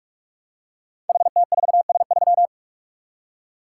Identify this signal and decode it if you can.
Morse code, heard as a keyed tone, about 44 wpm, 700 Hz. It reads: HT4S3
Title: Code Groups - 5 Letter:Numbers - 45wpm